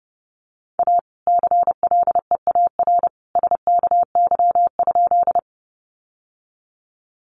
Morse code, heard as a keyed tone, about 30 wpm, 700 Hz. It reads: A CLEAR SKY?
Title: Sentences from Top 1200 Words 30wpm